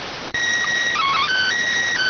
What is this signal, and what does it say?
SSTV (slow-scan television), PD 290.